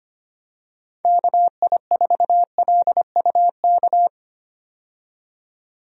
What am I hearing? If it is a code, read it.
Morse code, heard as a keyed tone, about 25 wpm, 700 Hz. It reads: KI4LUK
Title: USA Call Signs - 6 Characters - 25wpm